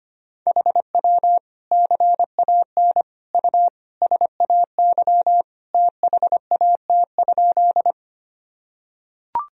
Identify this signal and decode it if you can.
Morse code, heard as a keyed tone, about 25 wpm, 700 Hz. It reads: HW CAN U SAY THAT?
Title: Sentences from Top 200 Words and Abbreviations 25wpm